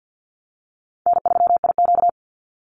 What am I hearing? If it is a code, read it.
Morse code, heard as a keyed tone, about 50 wpm, 700 Hz. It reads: N3IQ
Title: Call Signs Repeated 2x - 50wpm